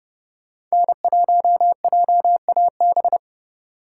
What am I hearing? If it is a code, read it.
Morse code, heard as a keyed tone, about 32 wpm, 700 Hz. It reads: N1JAB